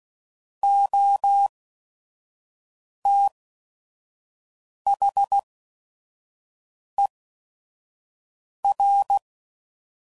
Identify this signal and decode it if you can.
Morse code, heard as a keyed tone, about 16 wpm, spaced as for 5 wpm, 800 Hz. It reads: OTHER
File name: Prefixes / By First Letter / Os